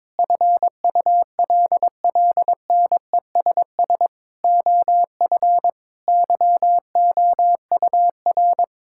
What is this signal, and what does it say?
Morse code, heard as a keyed tone, about 22 wpm, 700 Hz. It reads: FULLNESS OF YOUR
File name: Sets of 9 Words - 22wpm